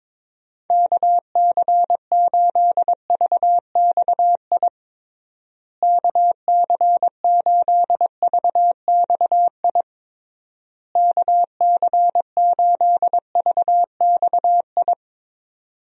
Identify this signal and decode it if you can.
Morse code, heard as a keyed tone, about 22 wpm, 700 Hz. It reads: KC8VXI KC8VXI KC8VXI
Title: USA Call Signs - 6 Characters - Repeated 3x - 22wpm